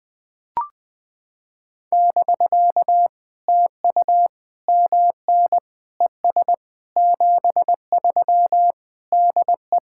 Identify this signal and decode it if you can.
Morse code, heard as a keyed tone, about 20 wpm, 700 Hz. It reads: <BK> TU MN ES 73 DE
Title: POTA QSO Practice - 20wpm